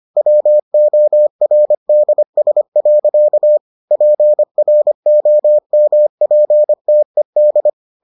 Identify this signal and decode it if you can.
Morse code, heard as a keyed tone, about 25 wpm, 600 Hz. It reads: WORDS. PROMPTED